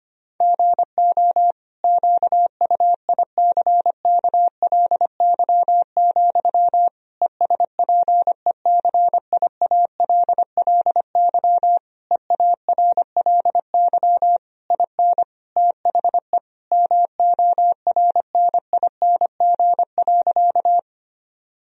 Morse code, heard as a keyed tone, about 25 wpm, 700 Hz. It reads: GO QUICKLY, ESPECIALLY EARLY IN THE MORNING.
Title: Sentences from Top 1000 Words - Review Part 4 - 25wpm